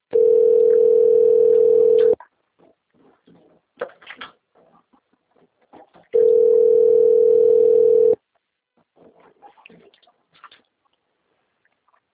Ringback tone (North America), 440 + 480 Hz.